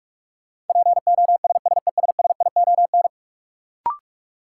Morse code, heard as a keyed tone, about 46 wpm, 700 Hz. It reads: POSSESSION
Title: Sentences from Top 1400 Words - New Word Drill - 45wpm